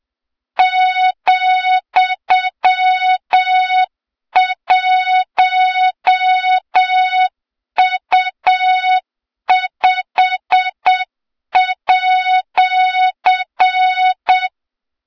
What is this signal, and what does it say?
Morse code, heard as a keyed tone, about 7 wpm, 3700 Hz, with noise. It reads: ,1U5@
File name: Morze-HA2MN / hang / 23 GYAKORLAT